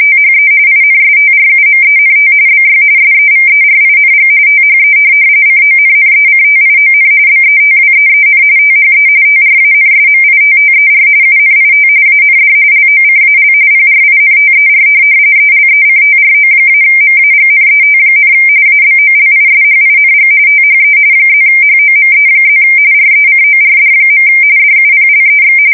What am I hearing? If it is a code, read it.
RTTY, about 45.45 baud. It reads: THE INSTRUMENT BY SEVERING THE CONNECTION BETWEEN THE TRANSMITTER UNIT AND LOCAL PRINTER ELECTRO-MAGNET, ABOLISHING THE +HOME+ RECORD. DUPLEX
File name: fsk170 / morkrum